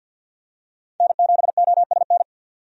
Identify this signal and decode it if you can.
Morse code, heard as a keyed tone, about 50 wpm, 700 Hz. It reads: N7OSN
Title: USA Call Signs - 5 Characters - 50wpm